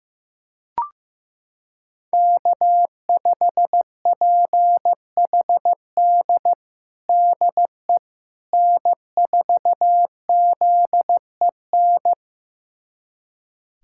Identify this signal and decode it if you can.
Morse code, heard as a keyed tone, about 15 wpm, 700 Hz. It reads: K5PHD DE N4ZEN
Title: USA Callsign DE Callsign - 5 Characters - 15wpm